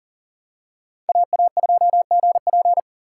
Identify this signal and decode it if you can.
Morse code, heard as a keyed tone, about 40 wpm, 700 Hz. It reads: AA2GP
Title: USA Call Signs - 4x Character Spacing - 40wpm